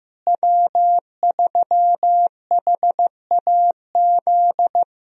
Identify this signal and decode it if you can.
Morse code, heard as a keyed tone, about 15 wpm, 700 Hz. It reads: W3HAZ
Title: USA Call Signs - 5 Characters - Repeated 3x - 15wpm